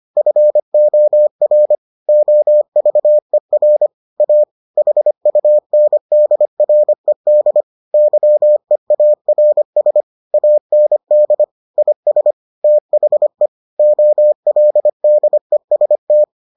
Morse code, heard as a keyed tone, about 25 wpm, 600 Hz. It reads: FOR OVER A HUNDRED YEARS AND IS THE OLDEST